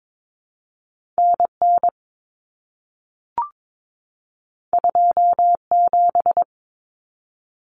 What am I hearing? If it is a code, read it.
Morse code, heard as a keyed tone, about 22 wpm, 700 Hz. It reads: NN 27
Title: Common Cut Numbers - 2-Digit 22wpm